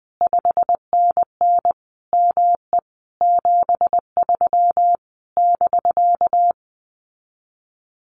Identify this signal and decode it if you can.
Morse code, heard as a keyed tone, about 20 wpm, 700 Hz. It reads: 5NN ME 73 <BK>
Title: Common QSO Phrases Encore 20wpm